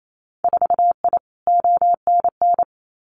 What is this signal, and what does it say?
Morse code, heard as a keyed tone, about 28 wpm, 700 Hz. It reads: 4I ONN
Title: Winter Field Day Exchanges Encore 28wpm